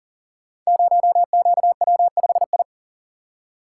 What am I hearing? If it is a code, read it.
Morse code, heard as a keyed tone, about 38 wpm, 700 Hz. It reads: 0QW5I
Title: Code Groups - 5 Letter:Numbers - 40wpm